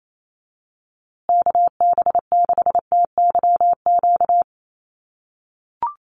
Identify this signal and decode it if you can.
Morse code, heard as a keyed tone, about 28 wpm, 700 Hz. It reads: KB6TYQ
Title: USA Call Signs Repeated 2x 28wpm